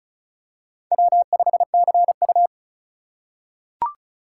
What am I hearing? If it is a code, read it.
Morse code, heard as a keyed tone, about 34 wpm, 700 Hz. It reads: W5CU
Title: USA Call Signs - 4 Characters - 4x Character Spacing - 35wpm